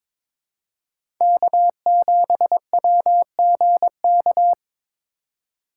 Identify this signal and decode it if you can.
Morse code, heard as a keyed tone, about 22 wpm, 700 Hz. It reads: K7WGK
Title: USA Call Signs - 5 Characters - 4x Character Spacing - 22wpm